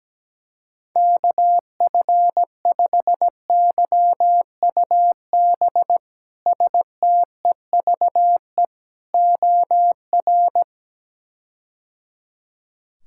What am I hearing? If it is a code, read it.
Morse code, heard as a keyed tone, about 17 wpm, 700 Hz. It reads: KF5YUB STEVE OR
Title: K1USN SST QSO Practice - 17wpm